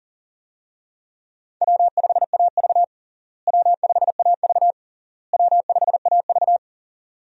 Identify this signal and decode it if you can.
Morse code, heard as a keyed tone, about 40 wpm, 700 Hz. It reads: W5AV W5AV W5AV
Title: USA Call Signs - 4 Characters - Repeated 3x - 40wpm